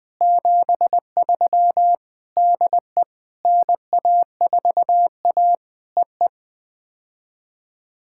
Morse code, heard as a keyed tone, about 18 wpm, 700 Hz. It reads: 73 DE NA4A EE
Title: POTA QSO Practice - 20wpm